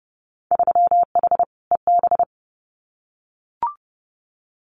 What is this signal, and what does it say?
Morse code, heard as a keyed tone, about 30 wpm, 700 Hz. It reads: 3H EB
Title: Winter Field Day Exchanges Encore 30wpm